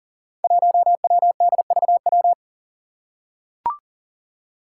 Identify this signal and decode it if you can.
Morse code, heard as a keyed tone, about 40 wpm, 690 Hz. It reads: 1WDVW
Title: Five Challenging Letter-Number Combinations 40wpm